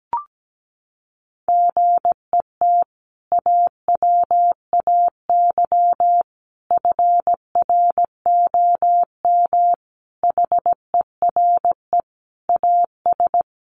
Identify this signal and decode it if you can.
Morse code, heard as a keyed tone, about 17 wpm, 700 Hz. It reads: GET AWAY FROM HERE AS
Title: Sentences from Top 600 Words 17wpm